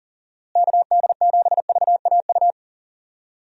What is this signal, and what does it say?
Morse code, heard as a keyed tone, about 40 wpm, 700 Hz. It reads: KD7VAU
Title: USA Call Signs - 6 Characters - Repeated 2x - 40wpm